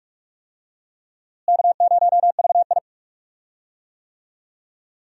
Morse code, heard as a keyed tone, about 42 wpm, 700 Hz. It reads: K0VI
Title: USA Call Signs - 4 Characters - Repeated 3x - 45wpm